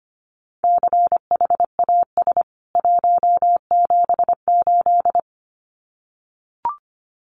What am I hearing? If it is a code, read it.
Morse code, heard as a keyed tone, about 25 wpm, 700 Hz. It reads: CHAS 178
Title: CWT Exchange 25wpm